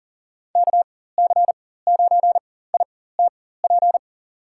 Morse code, heard as a keyed tone, about 40 wpm, spaced as for 19 wpm, 700 Hz. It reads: KC9ITP
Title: USA Call Signs - 6 Characters - 4x Character Spacing - 40wpm